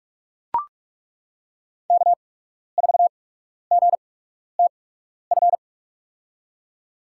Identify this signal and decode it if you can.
Morse code, heard as a keyed tone, about 42 wpm, spaced as for 12 wpm, 700 Hz. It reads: K4GTF
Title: USA Call Signs - 5 Characters - 8x Character Spacing - 45wpm